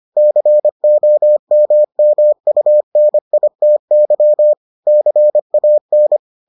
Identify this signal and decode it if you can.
Morse code, heard as a keyed tone, about 25 wpm, 600 Hz. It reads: COMMUNITY CAN